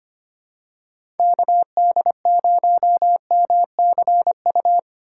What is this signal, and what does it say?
Morse code, heard as a keyed tone, about 25 wpm, 700 Hz. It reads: KD0MCU
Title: USA Call Signs - 6 Characters - 3x Character Spacing - 25wpm